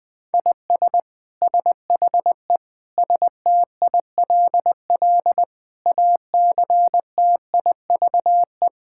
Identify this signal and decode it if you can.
Morse code, heard as a keyed tone, about 20 wpm, 700 Hz. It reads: IS SHE STILL ACTIVE
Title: Sentences from Top 1000 Words - Review Part 6 - 20wpm